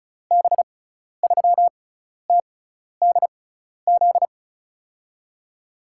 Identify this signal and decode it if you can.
Morse code, heard as a keyed tone, about 34 wpm, spaced as for 12 wpm, 700 Hz. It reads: B3TDZ
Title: USA Call Signs - 6 Characters - 6x Character Spacing - 35wpm